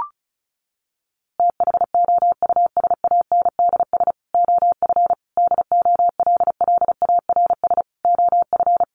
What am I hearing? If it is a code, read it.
Morse code, heard as a keyed tone, about 34 wpm, 700 Hz. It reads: THOUSANDS OF DOLLARS OF
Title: A Sound Of Thunder 35wpm